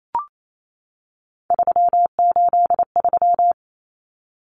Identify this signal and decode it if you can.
Morse code, heard as a keyed tone, about 28 wpm, 700 Hz. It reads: 383
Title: Numbers - 3-Digit 28wpm